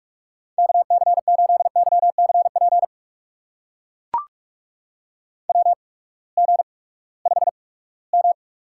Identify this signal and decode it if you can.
Morse code, heard as a keyed tone, about 44 wpm, spaced as for 22 wpm, 700 Hz. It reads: KC8YCP WG5M
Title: USA Call Signs - 8x Character Spacing - 45wpm